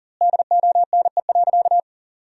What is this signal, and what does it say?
Morse code, heard as a keyed tone, about 40 wpm, 700 Hz. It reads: DONE.
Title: 2x Word-Spacing - Sentences from Top 500 Words - 40wpm